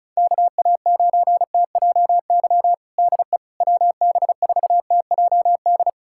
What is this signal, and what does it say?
Morse code, heard as a keyed tone, about 34 wpm, 700 Hz. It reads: KA9TJY DE WB4TJD
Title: USA Callsign DE Callsign 35wpm